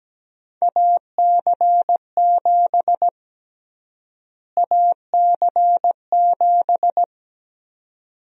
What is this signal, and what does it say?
Morse code, heard as a keyed tone, about 17 wpm, 700 Hz. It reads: AC7 AC7
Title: USA Call Sign Prefixes - Repeated 2x - 17wpm